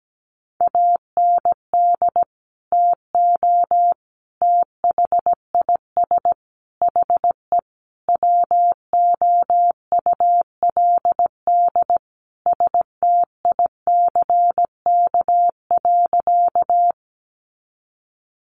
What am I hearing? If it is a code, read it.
Morse code, heard as a keyed tone, about 17 wpm, 700 Hz. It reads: AND TO THIS HE WOULD STICK.
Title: Sentences from Top 1200 Words 17wpm